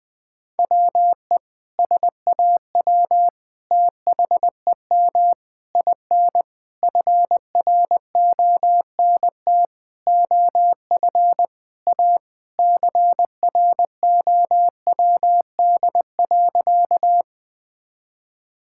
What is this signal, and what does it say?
Morse code, heard as a keyed tone, about 21 wpm, 700 Hz. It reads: WE SAW THEM IN FRONT OF A CROWD.
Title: Sentences with 4-Word Phrases - 20wpm